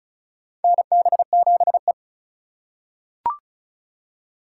Morse code, heard as a keyed tone, about 36 wpm, 700 Hz. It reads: NB7E
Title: USA Call Signs - 4 Characters - 4x Character Spacing - 35wpm